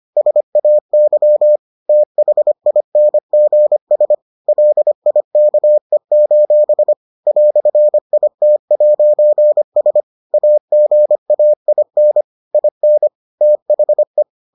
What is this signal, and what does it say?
Morse code, heard as a keyed tone, about 25 wpm, 600 Hz. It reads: SAY THINGS LIKE: "IT'S AGAIN IN THE